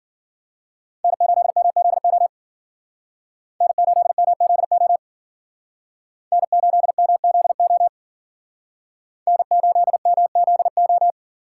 Morse code, heard as a keyed tone, about 54 wpm, 700 Hz. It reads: N8MZO N8MZO N8MZO N8MZO